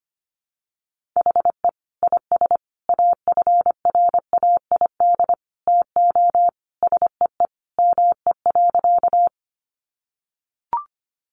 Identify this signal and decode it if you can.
Morse code, heard as a keyed tone, about 25 wpm, 700 Hz. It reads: HE IS AFRAID TO SEE ME.
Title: Sentences from Top 1500 Words 25wpm